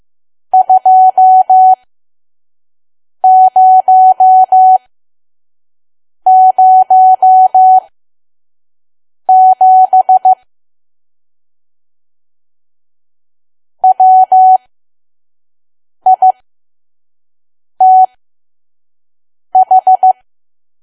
Morse code, heard as a keyed tone, about 15 wpm, spaced as for 5 wpm, 750 Hz. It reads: 2007 WITH
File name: Archive / 5 WPM files / PM